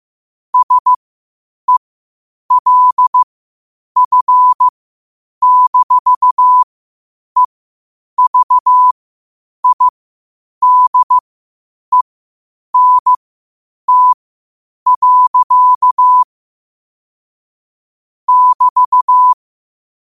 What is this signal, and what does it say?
Morse code, heard as a keyed tone, about 15 wpm, spaced as for 8 wpm, 1000 Hz. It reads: SELF-EVIDENT. <BT>
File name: qotdfiles / 8wpm